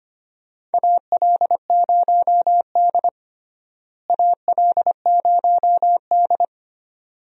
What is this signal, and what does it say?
Morse code, heard as a keyed tone, about 25 wpm, 700 Hz. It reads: AL0D AL0D
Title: USA Call Signs - 4 Characters - Repeated 2x - 25wpm